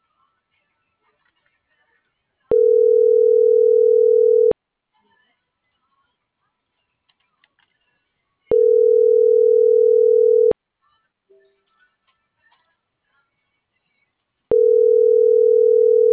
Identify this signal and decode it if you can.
Ringback tone (North America), 440 + 480 Hz.